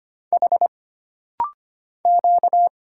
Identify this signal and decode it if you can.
Morse code, heard as a keyed tone, about 25 wpm, 700 Hz. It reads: H Q
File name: Single Character (Letter:Number:Punctuation:Prosign) - Rapid-Fire - 25wpm